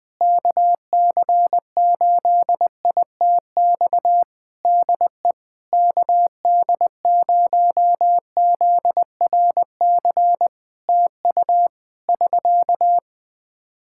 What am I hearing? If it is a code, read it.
Morse code, heard as a keyed tone, about 21 wpm, 700 Hz. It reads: KC8ITX DE KD0ZRC TU <SK>
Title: Practice QSO - Exchange 4 - Ending - 20wpm